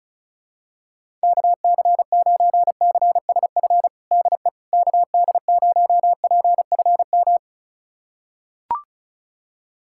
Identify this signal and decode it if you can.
Morse code, heard as a keyed tone, about 34 wpm, 700 Hz. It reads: KC9CSF DE KD0PFM
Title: USA Callsign DE Callsign - 6 Characters - 35wpm